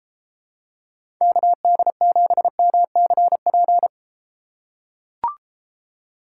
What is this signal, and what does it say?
Morse code, heard as a keyed tone, about 33 wpm, 700 Hz. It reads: KD7MCP